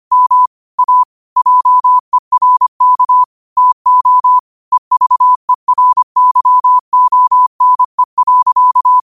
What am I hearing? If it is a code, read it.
Morse code, heard as a keyed tone, about 25 wpm, 1000 Hz. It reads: M A JERK TO EVERYONE.